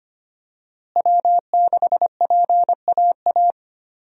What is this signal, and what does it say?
Morse code, heard as a keyed tone, about 25 wpm, 700 Hz. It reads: W6PAA
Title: USA Call Signs - 5 Characters - Repeated 2x - 25wpm